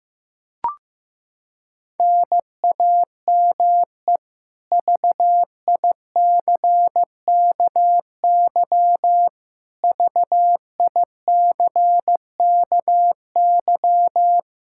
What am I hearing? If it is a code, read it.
Morse code, heard as a keyed tone, about 16 wpm, 700 Hz. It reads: NAME VICKY VICKY
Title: K1USN SST QSO Practice 15wpm